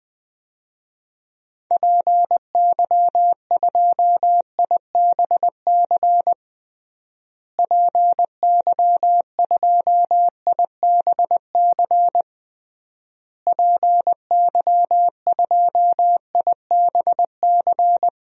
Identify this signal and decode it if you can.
Morse code, heard as a keyed tone, about 20 wpm, 700 Hz. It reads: PY2IBC PY2IBC PY2IBC
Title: DXCC-100 Call Signs - 6 Characters - Repeated 3x - 20wpm